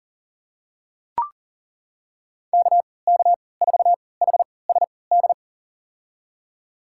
Morse code, heard as a keyed tone, about 40 wpm, spaced as for 23 wpm, 700 Hz. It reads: KK4HSD
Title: USA Call Signs - 3x Character Spacing - 40wpm